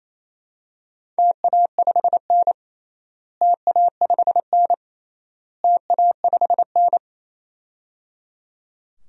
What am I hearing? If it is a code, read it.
Morse code, heard as a keyed tone, about 28 wpm, 700 Hz. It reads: TA5N TA5N TA5N
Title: DXCC-100 Call Signs - 4 Characters - Repeated 3x - 28wpm